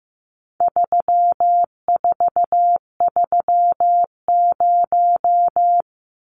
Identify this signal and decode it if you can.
Morse code, heard as a keyed tone, about 15 wpm, 700 Hz. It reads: 3430
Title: Numbers - 4-Digit 15wpm